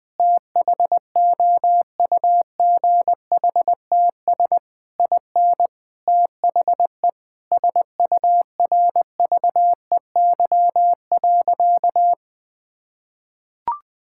Morse code, heard as a keyed tone, about 20 wpm, 700 Hz. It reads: THOUGHTS IN THE SURVEY.
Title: Sentences with 7-Word Phrases - 20wpm